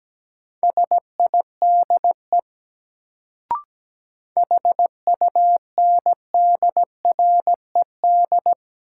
Morse code, heard as a keyed tone, about 17 wpm, 700 Hz. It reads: SIDE HUNDRED
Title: Sentences from Top 400 Words - New Word Drill - 17wpm